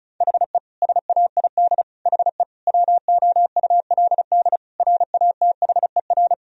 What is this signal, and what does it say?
Morse code, heard as a keyed tone, about 36 wpm, 700 Hz. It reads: HE SAID HE WOULD RATHER
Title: Sentences from Top 1000 Words - Review Part 4 - 35wpm